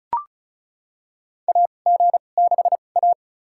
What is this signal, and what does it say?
Morse code, heard as a keyed tone, about 36 wpm, spaced as for 25 wpm, 700 Hz. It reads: AG6A
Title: USA Call Signs - 4 Characters - 2x Character Spacing - 35wpm